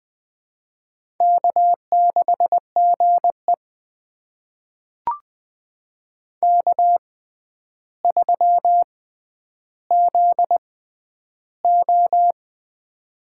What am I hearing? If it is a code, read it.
Morse code, heard as a keyed tone, about 21 wpm, spaced as for 10 wpm, 700 Hz. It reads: K6GE K3ZO
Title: USA Call Signs - 4 Characters - 6x Character Spacing - 20wpm